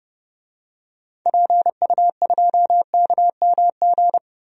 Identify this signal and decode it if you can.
Morse code, heard as a keyed tone, about 30 wpm, 700 Hz. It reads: PU2KMG
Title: DXCC-100 Call Signs - 6 Characters - 30wpm